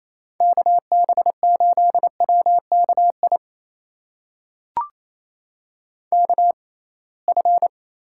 Morse code, heard as a keyed tone, about 28 wpm, spaced as for 21 wpm, 700 Hz. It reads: KB8WKI KF
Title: USA Call Signs - 6 Characters - 6x Character Spacing - 28wpm